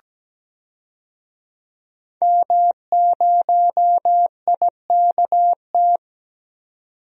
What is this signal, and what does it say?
Morse code, heard as a keyed tone, about 17 wpm, 700 Hz. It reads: M0IKT
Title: DXCC-100 Call Signs - 5 Characters - 17wpm